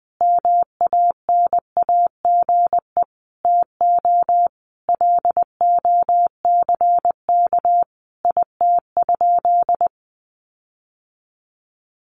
Morse code, heard as a keyed tone, about 20 wpm, 700 Hz. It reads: MANAGE TO LOCK IT?
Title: Sentences from Top 1400 Words 20wpm